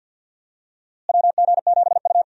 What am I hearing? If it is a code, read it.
Morse code, heard as a keyed tone, about 51 wpm, 700 Hz. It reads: WG7U
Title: USA Call Signs - 4 Characters - Repeated 3x - 50wpm